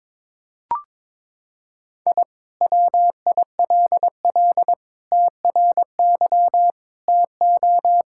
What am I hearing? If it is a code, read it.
Morse code, heard as a keyed tone, about 22 wpm, 700 Hz. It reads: I WILL TRY TO
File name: practice / 22wpm / Sentences with 3-Word Phrases - 22wpm